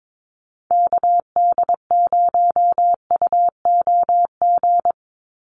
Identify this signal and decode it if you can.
Morse code, heard as a keyed tone, about 22 wpm, 700 Hz. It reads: KD0UOG
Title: USA Call Signs 22wpm